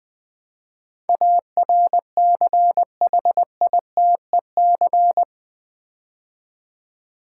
Morse code, heard as a keyed tone, about 20 wpm, 700 Hz. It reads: ARCHITEC
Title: Build A Word - 4 Letters At A Time - 12-Letter Words - 20wpm